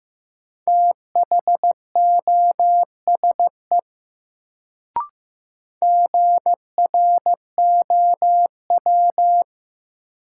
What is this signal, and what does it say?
Morse code, heard as a keyed tone, about 15 wpm, 700 Hz. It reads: THOSE GROW
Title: Sentences from Top 300 Words - New Word Drill - 15wpm